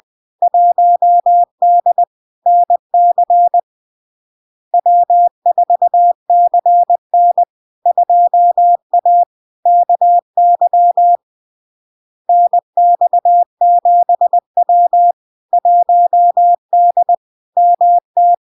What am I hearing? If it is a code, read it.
Morse code, heard as a keyed tone, about 20 wpm, 700 Hz. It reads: 1D NC W4CN 2A KY NX7W 1D MT
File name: LARK-FD-2022-Log-in-CW-20wpm-700-Hz